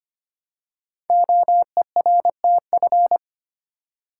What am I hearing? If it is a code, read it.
Morse code, heard as a keyed tone, about 25 wpm, 700 Hz. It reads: OERTF
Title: Code Groups - 5 Letters - 25wpm